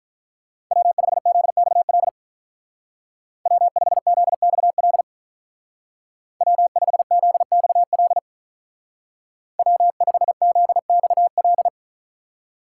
Morse code, heard as a keyed tone, about 44 wpm, 700 Hz. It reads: W5ZXL W5ZXL W5ZXL W5ZXL